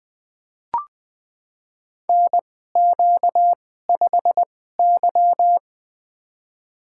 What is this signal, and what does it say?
Morse code, heard as a keyed tone, about 20 wpm, spaced as for 15 wpm, 700 Hz. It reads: NQ5Y
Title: USA Call Signs - 4 Characters - 2x Character Spacing - 20wpm